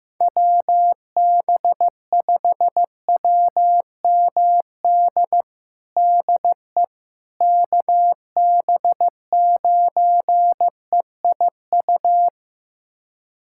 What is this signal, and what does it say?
Morse code, heard as a keyed tone, about 16 wpm, 700 Hz. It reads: WB5WMD DE KB9EIU
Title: USA Callsign DE Callsign - 6 Characters - 15wpm